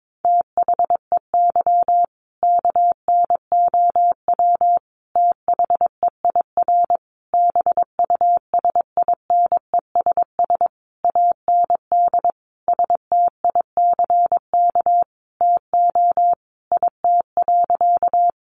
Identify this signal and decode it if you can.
Morse code, heard as a keyed tone, about 22 wpm, 700 Hz. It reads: THEY KNOW THEIR BUSINESS AND STICK TO IT.
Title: Sentences from Top 1200 Words 22wpm